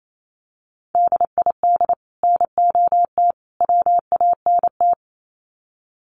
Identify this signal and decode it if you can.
Morse code, heard as a keyed tone, about 28 wpm, 700 Hz. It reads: DID NOT WANT
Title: Sets of 3 Words 28wpm